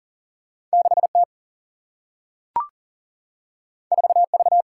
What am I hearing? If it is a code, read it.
Morse code, heard as a keyed tone, about 40 wpm, 700 Hz. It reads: 6T 4V
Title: Single-Letter Single-Number 40wpm